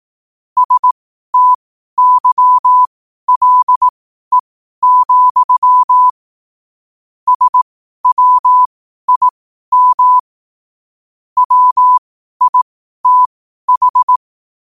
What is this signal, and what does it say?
Morse code, heard as a keyed tone, about 18 wpm, spaced as for 12 wpm, 1000 Hz. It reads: STYLE, SWIM WITH
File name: qotdfiles / 12wpm